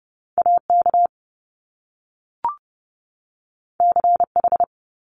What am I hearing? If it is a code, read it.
Morse code, heard as a keyed tone, about 30 wpm, 700 Hz. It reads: AK CH
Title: Top Two Letter Combinations in Words 30wpm